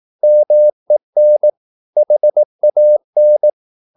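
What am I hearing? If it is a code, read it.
Morse code, heard as a keyed tone, about 18 wpm, 600 Hz. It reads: MEN HAN
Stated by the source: Slow (QRS) Morse practice at 18 wpm, CW med Gnister.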